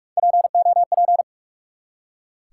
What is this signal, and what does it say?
Morse code, heard as a keyed tone, about 44 wpm, 700 Hz. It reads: POP
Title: Three Letter Words - 45wpm